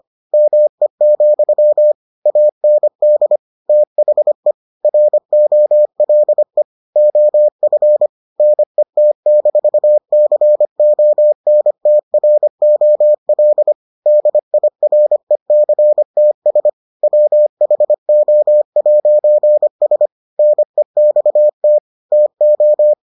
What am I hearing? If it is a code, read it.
Morse code, heard as a keyed tone, about 25 wpm, 600 Hz. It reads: ME, AND THE ROLE OF NET-CONTROL DIRECTS WHO'S NEXT TO